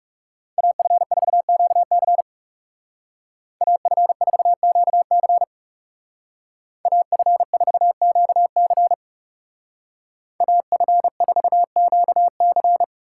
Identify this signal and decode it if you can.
Morse code, heard as a keyed tone, about 37 wpm, 700 Hz. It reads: AF4QC AF4QC AF4QC AF4QC
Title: Speed Racing - USA 2x2 Call Signs 30wpm